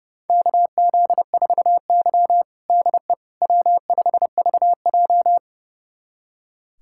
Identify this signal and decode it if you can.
Morse code, heard as a keyed tone, about 30 wpm, 700 Hz. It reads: KZ4Y DE W5VJ
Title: USA Callsign DE Callsign - 4 Characters - 30wpm